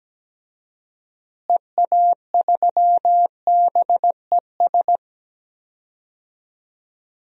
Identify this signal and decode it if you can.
Morse code, heard as a keyed tone, about 17 wpm, 700 Hz. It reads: EA3BES
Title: DXCC-100 Call Signs - 6 Characters - Repeated 3x - 17wpm